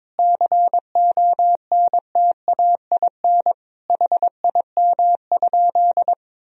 Morse code, heard as a keyed tone, about 23 wpm, 700 Hz. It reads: CONTAIN HIM?
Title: Sentences from Top 1000 Words - Review Part 2 - 22wpm